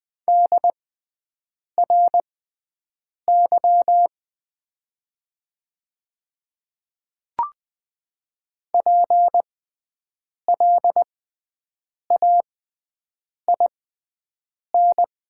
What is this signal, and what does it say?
Morse code, heard as a keyed tone, about 20 wpm, spaced as for 7 wpm, 700 Hz. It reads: DRY PLAIN
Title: 6x Character Spacing - Top 500 Words - 20wpm